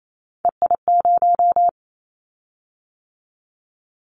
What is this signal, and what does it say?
Morse code, heard as a keyed tone, about 28 wpm, 700 Hz. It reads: EI0
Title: Callsign Prefixes for Top 50 DXCC - 28wpm